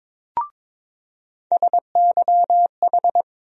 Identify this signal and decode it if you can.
Morse code, heard as a keyed tone, about 22 wpm, 700 Hz. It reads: SYH
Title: USA Call Sign Postfixes - 3 Letters 22wpm